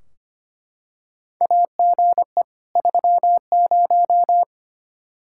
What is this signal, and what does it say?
Morse code, heard as a keyed tone, about 25 wpm, 700 Hz. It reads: AGE 30
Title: Practice QSO - Exchange 3 - More Info - 25wpm